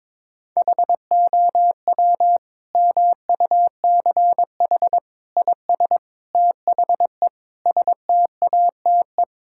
Morse code, heard as a keyed tone, about 22 wpm, 700 Hz. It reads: HOW MUCH IS THE STATE
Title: Sentences from Top 1000 Words - Review Part 5 - 22wpm